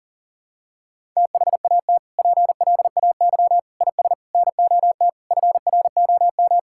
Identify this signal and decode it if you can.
Morse code, heard as a keyed tone, about 40 wpm, 700 Hz. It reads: THAT PLAY IS NOT FROM
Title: Sentences from Top 1000 Words - Review Part 5 - 40wpm